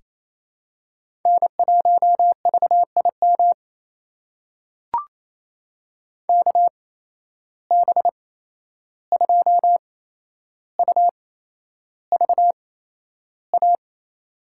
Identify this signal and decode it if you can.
Morse code, heard as a keyed tone, about 28 wpm, spaced as for 11 wpm, 700 Hz. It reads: N1VIM KB2UVA
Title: USA Call Signs - 8x Character Spacing - 28wpm